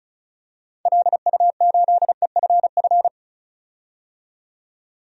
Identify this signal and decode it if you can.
Morse code, heard as a keyed tone, about 34 wpm, 700 Hz. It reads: LU8EFF
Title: DXCC-100 Call Signs - 6 Characters - 35wpm